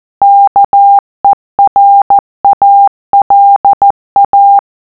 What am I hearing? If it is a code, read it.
Morse code, heard as a keyed tone, about 14 wpm, 800 Hz. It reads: KERALA